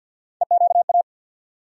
Morse code, heard as a keyed tone, about 47 wpm, 700 Hz. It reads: EQA
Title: USA Call Sign Postfixes - 3 Letters 50wpm